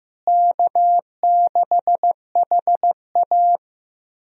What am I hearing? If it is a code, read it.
Morse code, heard as a keyed tone, about 15 wpm, 700 Hz. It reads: K6HA
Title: USA Call Signs - 4 Characters - Repeated 3x - 15wpm